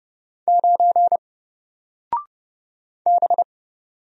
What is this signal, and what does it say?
Morse code, heard as a keyed tone, about 31 wpm, 700 Hz. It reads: 9 B
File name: Single-Letter-Number-30wpm